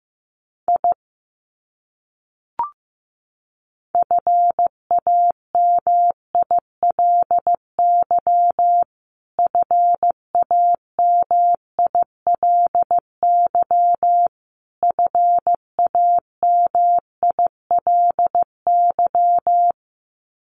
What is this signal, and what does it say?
Morse code, heard as a keyed tone, about 14 wpm, 700 Hz. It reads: I FAMILY FAMILY FAMILY
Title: Top 500 Words Repeated 15wpm